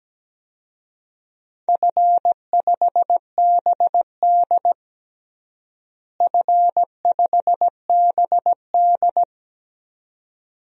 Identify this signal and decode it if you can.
Morse code, heard as a keyed tone, about 17 wpm, 700 Hz. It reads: F5BD F5BD
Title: DXCC-100 Call Signs - 4 Characters - Repeated 2x - 17wpm